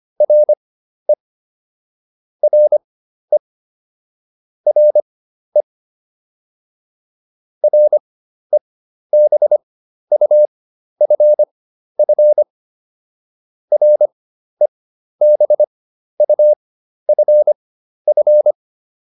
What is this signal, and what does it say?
Morse code, heard as a keyed tone, about 25 wpm, spaced as for 12 wpm, 600 Hz. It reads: RE RE RE REBUFF REBUFF